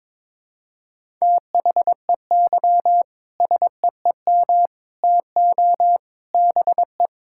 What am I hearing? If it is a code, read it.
Morse code, heard as a keyed tone, about 22 wpm, 700 Hz. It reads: THEY SEEM TO BE
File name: practice / 22wpm / Sentences with 3-Word Phrases - 22wpm